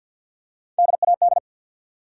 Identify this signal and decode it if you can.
Morse code, heard as a keyed tone, about 49 wpm, 700 Hz. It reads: DAD
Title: Three Letter Words - 50wpm